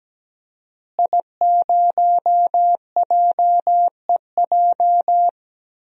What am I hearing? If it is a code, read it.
Morse code, heard as a keyed tone, about 17 wpm, 700 Hz. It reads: I0JEJ
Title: DXCC-100 Call Signs - 5 Characters - Repeated 2x - 17wpm